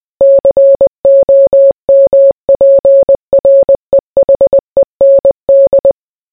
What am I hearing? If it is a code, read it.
Morse code, heard as a keyed tone, about 20 wpm, 550 Hz. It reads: COMPREHEND